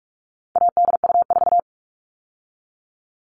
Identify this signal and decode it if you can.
Morse code, heard as a keyed tone, about 44 wpm, 700 Hz. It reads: ADU4
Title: All Cut Numbers - 4-Digit 45wpm